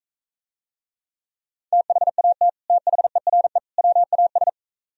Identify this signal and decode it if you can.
Morse code, heard as a keyed tone, about 44 wpm, 700 Hz. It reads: THAT THERE WAS